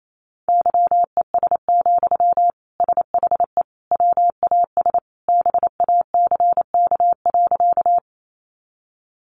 Morse code, heard as a keyed tone, about 28 wpm, 700 Hz. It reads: YES, SHE WAS BACK.
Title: Sentences from Top 500 Words 28wpm